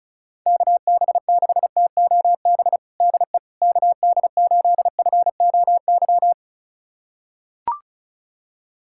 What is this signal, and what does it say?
Morse code, heard as a keyed tone, about 36 wpm, 700 Hz. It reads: KB6TOB DE KD8FOY
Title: USA Callsign DE Callsign 35wpm